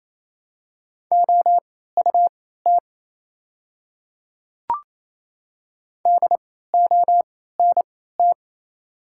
Morse code, heard as a keyed tone, about 28 wpm, spaced as for 16 wpm, 700 Hz. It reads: OUT DONT
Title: 3x Character Spacing - Top 300 Words - 28wpm